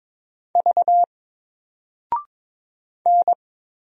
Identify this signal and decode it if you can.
Morse code, heard as a keyed tone, about 22 wpm, 700 Hz. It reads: V N
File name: Single-Letter-Number-22wpm